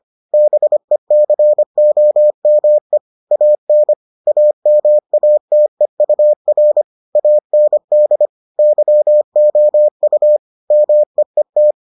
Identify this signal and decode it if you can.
Morse code, heard as a keyed tone, about 25 wpm, 600 Hz. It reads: BECOME AN AMATEUR AND YOU MEET